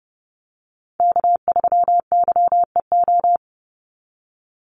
Morse code, heard as a keyed tone, about 30 wpm, 700 Hz. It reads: K3YEO
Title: Call Signs - 30wpm